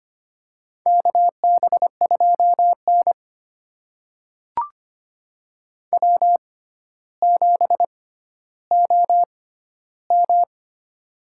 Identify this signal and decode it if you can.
Morse code, heard as a keyed tone, about 25 wpm, spaced as for 13 wpm, 700 Hz. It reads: KB2N W7OM
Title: USA Call Signs - 4 Characters - 6x Character Spacing - 25wpm